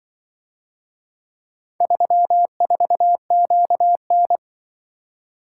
Morse code, heard as a keyed tone, about 24 wpm, 700 Hz. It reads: 34QN
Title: Speed-Racing - Four Letter-Number Combinations - 20wpm